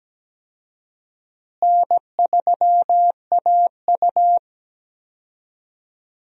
Morse code, heard as a keyed tone, about 17 wpm, 700 Hz. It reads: N3AU
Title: USA Call Signs - 4 Characters - Repeated 3x - 17wpm